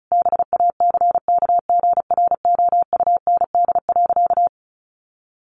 Morse code, heard as a keyed tone, about 34 wpm, 700 Hz. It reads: BACKGROUND.
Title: Sentences from Top 1300 Words 35wpm